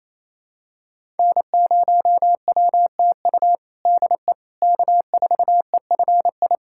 Morse code, heard as a keyed tone, about 28 wpm, 700 Hz. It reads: N0WTU DE K4EFI
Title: USA Callsign DE Callsign - 5 Characters - 28wpm